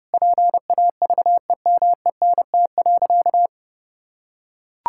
Morse code, heard as a keyed tone, about 30 wpm, 700 Hz. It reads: PAVEMENT.
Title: Pangram Challenge v2 - 30wpm